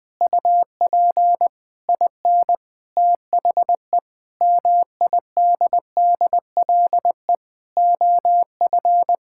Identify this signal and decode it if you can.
Morse code, heard as a keyed tone, about 21 wpm, 700 Hz. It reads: UP IN THE MIDDLE OF
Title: Sentences with 8-Word Phrases - Spoken Beginning - 20wpm